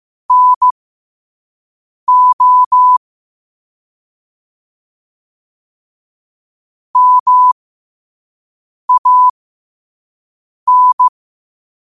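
Morse code, heard as a keyed tone, about 15 wpm, spaced as for 5 wpm, 1000 Hz. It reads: NO MAN
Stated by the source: Morse code quote of the day sent at 5 WPM.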